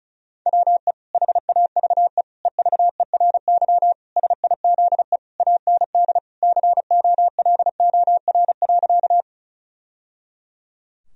Morse code, heard as a keyed tone, about 34 wpm, 700 Hz. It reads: WE HAVE EVERY SIZE AND COLOR.
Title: Sentences from Top 1000 Words - Review Part 2 - 35wpm